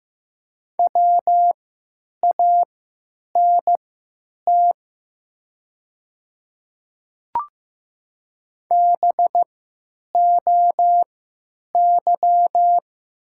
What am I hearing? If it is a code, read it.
Morse code, heard as a keyed tone, about 15 wpm, spaced as for 9 wpm, 700 Hz. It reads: WANT BOY
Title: 3x Character Spacing - Top 300 Words - 15wpm